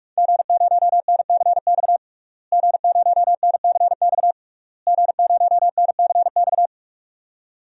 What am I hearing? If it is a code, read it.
Morse code, heard as a keyed tone, about 46 wpm, 700 Hz. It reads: G0NCX G0NCX G0NCX
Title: DXCC-100 Call Signs - 5 Characters - Repeated 3x - 45wpm